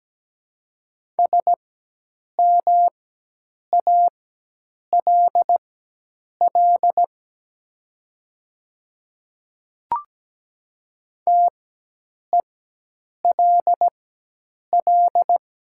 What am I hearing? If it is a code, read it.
Morse code, heard as a keyed tone, about 17 wpm, spaced as for 8 wpm, 700 Hz. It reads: SMALL TELL
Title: 4x Character Spacing - Top 300 Words - 17wpm